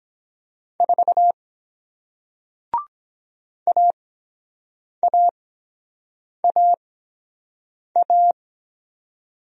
Morse code, heard as a keyed tone, about 23 wpm, spaced as for 7 wpm, 700 Hz. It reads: 4 AAAA